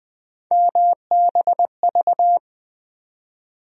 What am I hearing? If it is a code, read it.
Morse code, heard as a keyed tone, about 21 wpm, 700 Hz. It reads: MBV
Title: USA Call Sign Postfixes - 3 Letters 20wpm